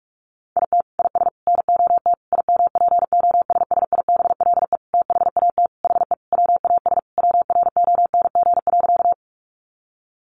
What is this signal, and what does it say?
Morse code, heard as a keyed tone, about 44 wpm, 700 Hz. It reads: IT IS NOT IMPOSSIBLE THAT HE WAS WRONG.
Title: Sentences from Top 1100 Words 45wpm